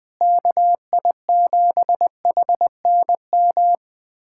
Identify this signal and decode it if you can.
Morse code, heard as a keyed tone, about 20 wpm, 700 Hz. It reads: KI7HNM
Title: USA Callsign DE Callsign - 6 Characters - 20wpm